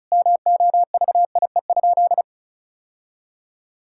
Morse code, heard as a keyed tone, about 34 wpm, 700 Hz. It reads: MOVIE?
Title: Sentences from Top 1000 Words - Review Part 5 - 35wpm